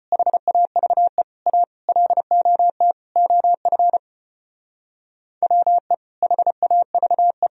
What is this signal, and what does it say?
Morse code, heard as a keyed tone, about 29 wpm, 700 Hz. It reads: HAVE A LOT OF WE HAVE
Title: Speed-Racing - Sets of 5 Words 25wpm